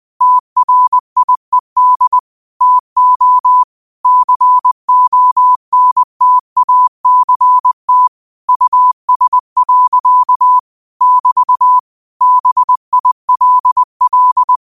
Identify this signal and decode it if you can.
Morse code, heard as a keyed tone, about 20 wpm, 1000 Hz. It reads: TRIED TO CONTACT US. <BT> BILL